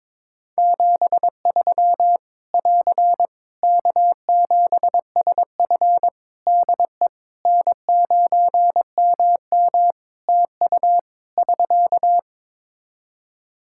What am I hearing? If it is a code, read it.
Morse code, heard as a keyed tone, about 22 wpm, 700 Hz. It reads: 73 <AR> K7SF DE N9MM TU <SK>
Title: K1USN SST QSO Practice 22wpm